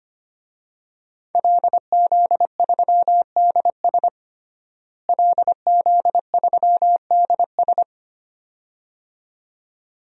Morse code, heard as a keyed tone, about 25 wpm, 700 Hz. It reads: LZ3DS LZ3DS
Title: DXCC-100 Call Signs - 5 Characters - Repeated 2x - 25wpm